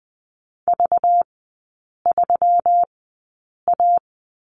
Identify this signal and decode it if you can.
Morse code, heard as a keyed tone, about 20 wpm, spaced as for 8 wpm, 700 Hz. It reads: V3A
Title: 191 Lesson 31e - Letter v - Three Characters 20wpm